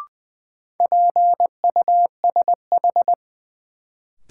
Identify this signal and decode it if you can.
Morse code, heard as a keyed tone, about 20 wpm, 700 Hz. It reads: PUSH
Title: Sentences from Top 1300 Words - New Word Drill Alt - 20wpm